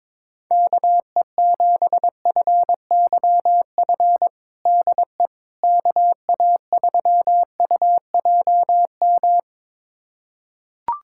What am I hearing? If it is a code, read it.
Morse code, heard as a keyed tone, about 22 wpm, 700 Hz. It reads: KE7FYF DE KA3UJM
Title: USA Callsign DE Callsign - 6 Characters - 22wpm